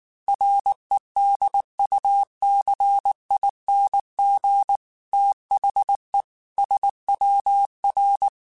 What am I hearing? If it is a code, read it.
Morse code, heard as a keyed tone, about 19 wpm, 800 Hz. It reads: REDUCING THE SWR
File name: QSOs / QSOs-mp3s / WD0FBT 1@18